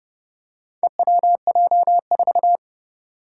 Morse code, heard as a keyed tone, about 32 wpm, 700 Hz. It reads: EWJ4
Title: Four Challenging Letter-Number Combinations 30wpm